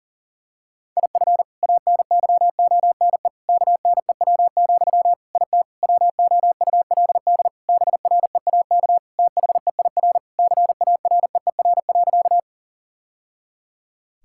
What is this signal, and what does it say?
Morse code, heard as a keyed tone, about 40 wpm, 700 Hz. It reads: IF ANYONE KNEW, IT WOULD BREAK THEIR CAREER.
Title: Sentences from Top 1000 Words - Review Part 6 - 40wpm